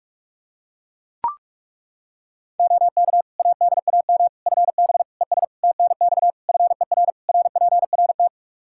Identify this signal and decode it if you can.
Morse code, heard as a keyed tone, about 41 wpm, 700 Hz. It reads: OK ADAM FB ES TNX FER RPRT
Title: Practice QSO - Exchange 2 - Info - 45wpm